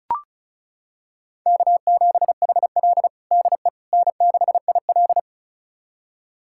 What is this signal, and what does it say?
Morse code, heard as a keyed tone, about 36 wpm, 700 Hz. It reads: K7HL DE N6IL
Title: USA Callsign DE Callsign - 4 Characters - 35wpm